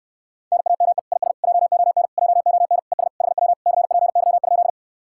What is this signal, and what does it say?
Morse code, heard as a keyed tone, about 69 wpm, 700 Hz. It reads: NAME IS JON JON ES HW CPY?